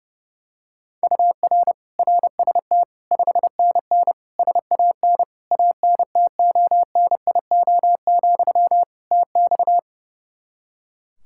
Morse code, heard as a keyed tone, about 31 wpm, 700 Hz. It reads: UR RST 5NN SAN ANTONIO, TX
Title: Practice QSO - Exchange 1 - Beginning - 30wpm